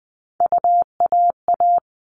Morse code, heard as a keyed tone, about 20 wpm, 700 Hz. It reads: UAA
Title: All Cut Numbers - 3-Digit 20wpm